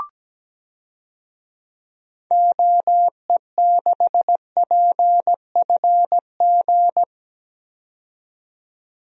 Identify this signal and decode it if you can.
Morse code, heard as a keyed tone, about 17 wpm, 700 Hz. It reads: OE6PFG
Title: DXCC-100 Call Signs - 6 Characters - 17wpm